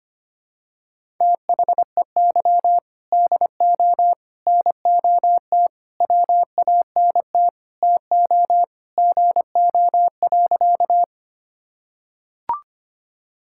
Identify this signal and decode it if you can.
Morse code, heard as a keyed tone, about 25 wpm, 700 Hz. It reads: THEY DO NOT WANT TO GO.
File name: practice / 25wpm / Sentences with 4-Word Phrases - 25wpm